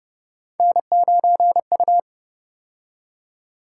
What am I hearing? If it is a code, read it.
Morse code, heard as a keyed tone, about 30 wpm, 700 Hz. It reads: N9U
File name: Speed-Racing - Three Letter-Number Combinations - 25wpm